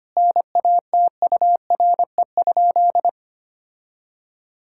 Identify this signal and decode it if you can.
Morse code, heard as a keyed tone, about 25 wpm, 700 Hz. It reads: NATURE?
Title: Sentences from Top 1000 Words - Review Part 5 - 25wpm